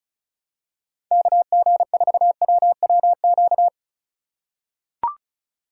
Morse code, heard as a keyed tone, about 36 wpm, 700 Hz. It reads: KG4WWQ
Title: USA Call Signs - 6 Characters - 3x Character Spacing - 35wpm